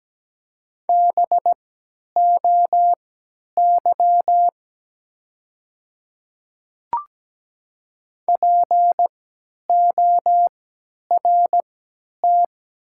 Morse code, heard as a keyed tone, about 17 wpm, spaced as for 10 wpm, 700 Hz. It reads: BOY PORT
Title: 3x Character Spacing - Top 200 Words - 17wpm